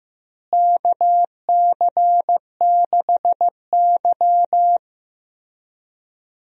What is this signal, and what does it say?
Morse code, heard as a keyed tone, about 15 wpm, 700 Hz. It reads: KC6Y
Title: USA Call Signs - 4 Characters - Repeated 3x - 15wpm